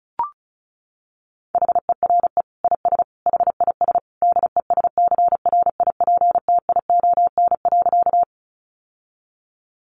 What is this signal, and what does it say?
Morse code, heard as a keyed tone, about 34 wpm, 700 Hz. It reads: HERE IS HIS DESCRIPTION.
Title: Sentences from Top 1300 Words 35wpm